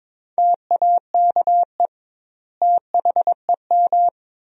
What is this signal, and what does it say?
Morse code, heard as a keyed tone, about 21 wpm, 700 Hz. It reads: TAKE THEM
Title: 2x Word-Spacing - Sentences from Top 200 Words - 22wpm